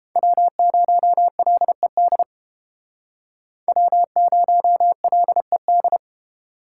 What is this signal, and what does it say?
Morse code, heard as a keyed tone, about 31 wpm, 700 Hz. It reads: W0LED W0LED